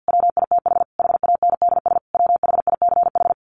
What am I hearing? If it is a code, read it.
Morse code, heard as a keyed tone, about 50 wpm, 700 Hz. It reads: WITH HANDS WHICH
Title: A Sound Of Thunder 50wpm